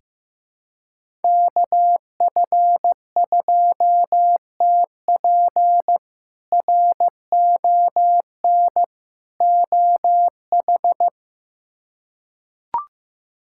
Morse code, heard as a keyed tone, about 15 wpm, 700 Hz. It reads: KF2TP RON OH
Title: K1USN SST QSO Practice - 15wpm